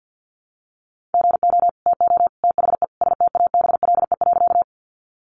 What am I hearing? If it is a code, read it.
Morse code, heard as a keyed tone, about 50 wpm, 700 Hz. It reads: GO TO THE STABLE.
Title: Sentences from Top 1500 Words 50wpm